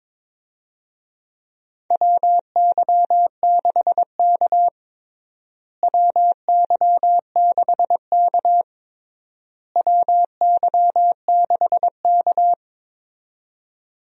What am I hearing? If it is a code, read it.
Morse code, heard as a keyed tone, about 22 wpm, 700 Hz. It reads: WY6K WY6K WY6K
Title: USA Call Signs - 4 Characters - Repeated 3x - 22wpm